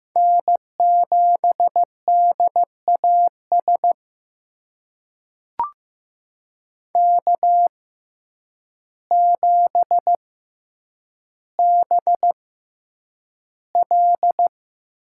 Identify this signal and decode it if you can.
Morse code, heard as a keyed tone, about 15 wpm, spaced as for 8 wpm, 700 Hz. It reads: N7DAS K7BL
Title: USA Call Signs - 5 Characters - 6x Character Spacing - 15wpm